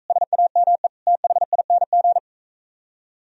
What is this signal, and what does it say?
Morse code, heard as a keyed tone, about 43 wpm, 700 Hz. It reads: SAME THING